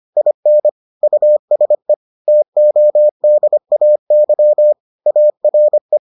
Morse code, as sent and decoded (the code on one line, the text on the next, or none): .. -. / ..- ... / - --- -.. .- -.-- / .- .-. .
IN USE TODAY ARE